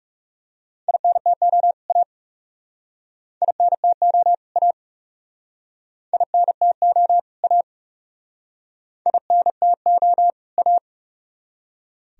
Morse code, heard as a keyed tone, about 38 wpm, 700 Hz. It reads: INTO A INTO A INTO A INTO A